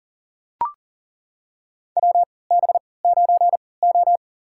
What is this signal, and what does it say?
Morse code, heard as a keyed tone, about 40 wpm, spaced as for 23 wpm, 700 Hz. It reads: WB9O
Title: USA Call Signs - 4 Characters - 3x Character Spacing - 40wpm